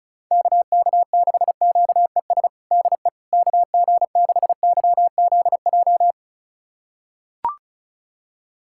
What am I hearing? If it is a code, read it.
Morse code, heard as a keyed tone, about 36 wpm, 700 Hz. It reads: KK6QES DE KG6YZJ
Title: USA Callsign DE Callsign - 6 Characters - 35wpm